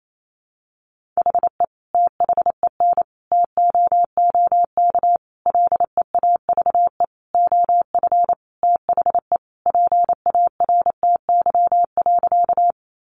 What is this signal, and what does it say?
Morse code, heard as a keyed tone, about 28 wpm, 700 Hz. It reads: HE THEN TOOK LEAVE OF THE PARTY.
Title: Sentences from Top 1400 Words 28wpm